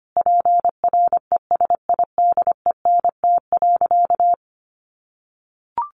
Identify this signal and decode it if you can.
Morse code, heard as a keyed tone, about 25 wpm, 700 Hz. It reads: PRESIDENT.
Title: Sentences from Top 1500 Words 25wpm